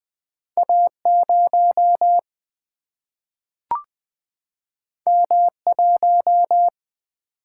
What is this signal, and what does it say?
Morse code, heard as a keyed tone, about 20 wpm, 700 Hz. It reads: A0 M1
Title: Single-Letter Single-Number 20wpm